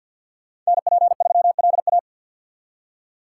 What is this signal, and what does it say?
Morse code, heard as a keyed tone, about 61 wpm, 700 Hz. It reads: NP3LA
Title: Speed Racing - USA Call Signs 50wpm